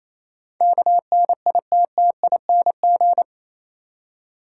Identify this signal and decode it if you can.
Morse code, heard as a keyed tone, about 27 wpm, 690 Hz. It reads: KNITTING
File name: Easy Words - Speed-Racing with Hints - 22wpm